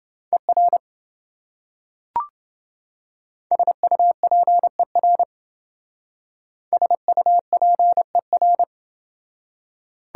Morse code, heard as a keyed tone, about 28 wpm, 700 Hz. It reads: ER SUPER SUPER